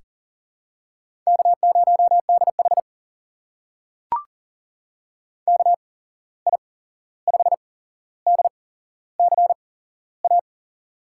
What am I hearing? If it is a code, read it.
Morse code, heard as a keyed tone, about 37 wpm, spaced as for 15 wpm, 700 Hz. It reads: K0DH KI5DCA
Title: USA Call Signs - 8x Character Spacing - 40wpm